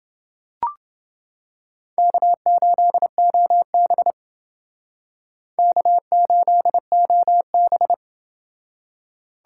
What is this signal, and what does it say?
Morse code, heard as a keyed tone, about 31 wpm, 700 Hz. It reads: K8OB K8OB